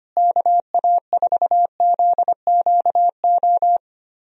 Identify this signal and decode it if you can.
Morse code, heard as a keyed tone, about 25 wpm, 700 Hz. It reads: KA4ZQO
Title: USA Call Signs - 6 Characters - Repeated 3x - 25wpm